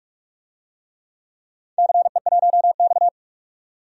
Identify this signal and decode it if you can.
Morse code, heard as a keyed tone, about 44 wpm, 700 Hz. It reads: CE1X
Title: DXCC-100 Call Signs - 4 Characters - 45wpm